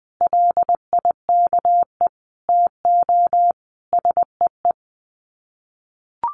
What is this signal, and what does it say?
Morse code, heard as a keyed tone, about 20 wpm, 700 Hz. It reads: LIKE TO SEE
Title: Sets of 5 Words Encore2 20wpm